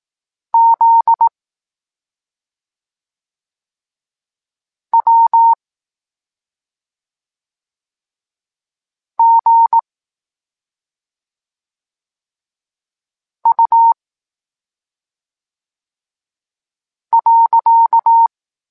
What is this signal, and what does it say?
Morse code, heard as a keyed tone, about 18 wpm, 920 Hz. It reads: Z W G U .
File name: Tekst-26 1